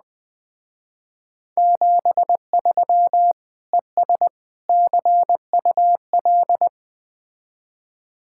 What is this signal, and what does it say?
Morse code, heard as a keyed tone, about 21 wpm, 700 Hz. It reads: 73 ES CUL
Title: Speed-Racing Common QSO Phrases - 17wpm